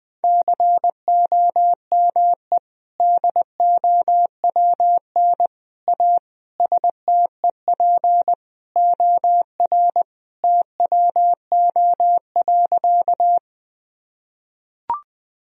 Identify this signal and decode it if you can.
Morse code, heard as a keyed tone, about 20 wpm, 700 Hz. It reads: COME DOWN A STEP OR TWO.
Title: Sentences from Top 1000 Words - Review Part 2 - 20wpm